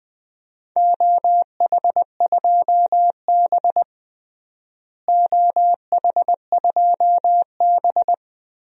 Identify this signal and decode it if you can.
Morse code, heard as a keyed tone, about 20 wpm, 700 Hz. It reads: OH2B OH2B
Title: DXCC-100 Call Signs - 4 Characters - Repeated 2x - 20wpm